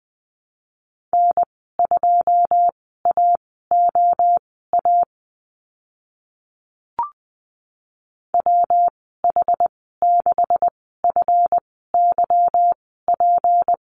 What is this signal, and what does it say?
Morse code, heard as a keyed tone, about 20 wpm, spaced as for 14 wpm, 700 Hz. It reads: N2AOA WH6FYP
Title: 161 Lesson 26g - Letter g - Call Signs 20wpm